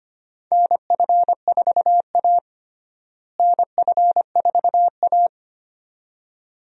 Morse code, heard as a keyed tone, about 25 wpm, 700 Hz. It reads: NF4A NF4A
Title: USA Call Signs - 4 Characters - Repeated 2x - 25wpm